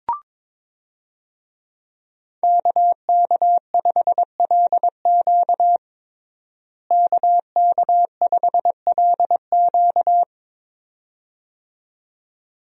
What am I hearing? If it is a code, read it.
Morse code, heard as a keyed tone, about 22 wpm, 700 Hz. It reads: KK5LQ KK5LQ
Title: USA Call Signs - 5 Characters - Repeated 2x - 22wpm